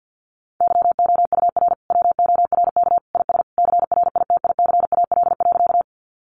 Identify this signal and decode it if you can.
Morse code, heard as a keyed tone, about 50 wpm, 700 Hz. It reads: YOUR WORK IS CRITICAL.
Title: Sentences from Top 1100 Words 50wpm